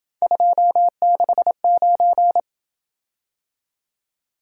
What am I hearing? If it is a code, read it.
Morse code, heard as a keyed tone, about 27 wpm, 700 Hz. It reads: 269